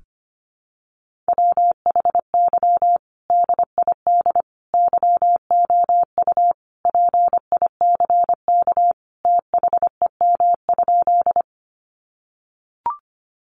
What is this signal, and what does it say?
Morse code, heard as a keyed tone, about 25 wpm, 700 Hz. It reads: WHY DID YOU PICK THEM?
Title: Sentences from Top 1000 Words 25wpm